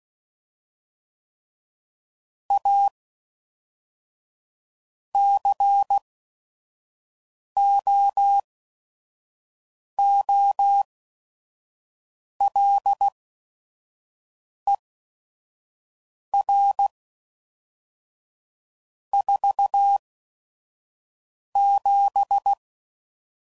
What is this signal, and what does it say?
Morse code, heard as a keyed tone, about 16 wpm, spaced as for 5 wpm, 800 Hz. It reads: A COOLER 47
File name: CW Code Groups @5WPM / Weather Sets / Weather 5 5